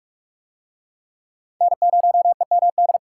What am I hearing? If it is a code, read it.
Morse code, heard as a keyed tone, about 44 wpm, 700 Hz. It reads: N0EMD